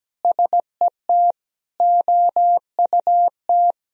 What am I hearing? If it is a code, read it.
Morse code, heard as a keyed tone, about 17 wpm, 700 Hz. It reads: SET OUT
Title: Sentences from Top 1000 Words - Review Part 1 - 17wpm